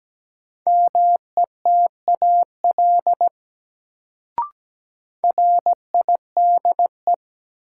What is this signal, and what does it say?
Morse code, heard as a keyed tone, about 17 wpm, 700 Hz. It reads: METAL RIDE
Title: Sentences from Top 1000 Words - Review Part 6 - New Word Drill - 17wpm